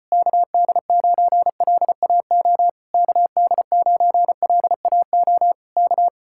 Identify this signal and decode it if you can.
Morse code, heard as a keyed tone, about 34 wpm, 700 Hz. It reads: KD9LAO KD9LAO K